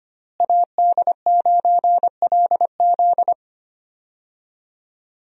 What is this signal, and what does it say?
Morse code, heard as a keyed tone, about 25 wpm, 700 Hz. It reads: AD9LZ
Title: USA Call Signs - 5 Characters - Repeated 3x - 25wpm